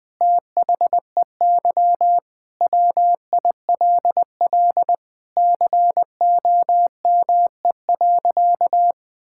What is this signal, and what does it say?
Morse code, heard as a keyed tone, about 20 wpm, 700 Hz. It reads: THEY WILL COME.
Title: Sentences with 7-Word Phrases - Spoken Beginning - 20wpm